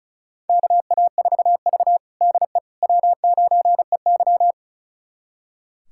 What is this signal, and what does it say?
Morse code, heard as a keyed tone, about 34 wpm, 700 Hz. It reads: KA4V DE W9EY
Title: USA Callsign DE Callsign - 4 Characters - 35wpm